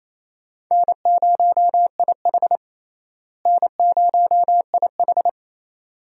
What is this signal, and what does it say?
Morse code, heard as a keyed tone, about 27 wpm, 700 Hz. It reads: N0IH N0IH
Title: USA Call Signs - 4 Characters - Repeated 2x - 28wpm